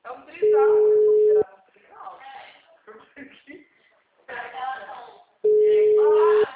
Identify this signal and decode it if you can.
Ringback tone (Europe, 425 Hz), with speech.